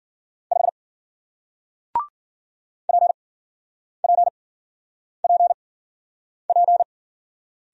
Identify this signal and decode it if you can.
Morse code, heard as a keyed tone, about 50 wpm, spaced as for 9 wpm, 700 Hz. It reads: 5 PPPP